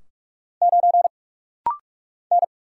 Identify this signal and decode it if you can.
Morse code, heard as a keyed tone, about 44 wpm, 690 Hz. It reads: 9 N
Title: Single Letter-Number - Rapid-Fire 45wpm